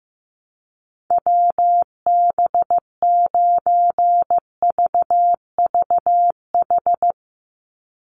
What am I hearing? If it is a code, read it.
Morse code, heard as a keyed tone, about 15 wpm, 700 Hz. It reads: WB9VVH
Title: USA Call Signs 15wpm